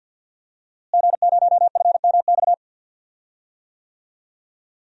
Morse code, heard as a keyed tone, about 52 wpm, 700 Hz. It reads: G0FMX
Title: DXCC-100 Call Signs - 5 Characters - Repeated 2x - 50wpm